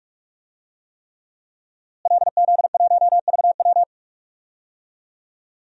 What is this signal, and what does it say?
Morse code, heard as a keyed tone, about 41 wpm, 700 Hz. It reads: LZ1VW
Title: DXCC-100 Call Signs - 5 Characters - Repeated 3x - 45wpm